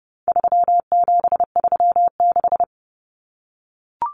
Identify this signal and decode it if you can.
Morse code, heard as a keyed tone, about 30 wpm, 700 Hz. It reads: 3736
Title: Numbers - 4-Digit 30wpm